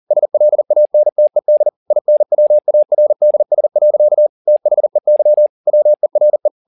Morse code, heard as a keyed tone, about 40 wpm, 600 Hz. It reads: SLANTED INWARDS. THEY WERE
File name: the old man and the sea 2 40wpm